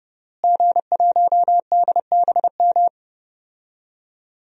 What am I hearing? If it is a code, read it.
Morse code, heard as a keyed tone, about 29 wpm, 700 Hz. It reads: G1DBM
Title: DXCC-100 Call Signs - 5 Characters - Repeated 3x - 30wpm